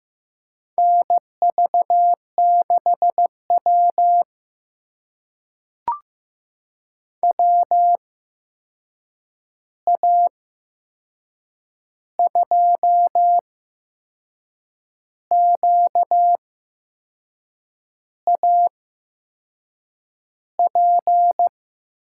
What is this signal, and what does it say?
Morse code, heard as a keyed tone, about 15 wpm, spaced as for 6 wpm, 700 Hz. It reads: NV6W WA2QAP
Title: USA Call Signs - 8x Character Spacing - 15wpm